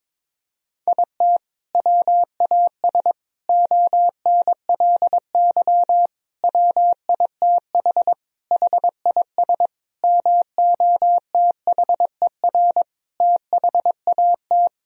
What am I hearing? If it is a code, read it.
Morse code, heard as a keyed tone, about 22 wpm, 700 Hz. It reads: IT WAS ONLY WITH HIS MOTHER THAT
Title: Sentences from Top 1000 Words - Review Part 1 - 22wpm